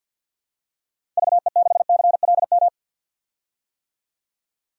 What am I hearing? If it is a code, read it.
Morse code, heard as a keyed tone, about 53 wpm, 700 Hz. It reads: VE6CLM
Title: DXCC-100 Call Signs - 6 Characters - 50wpm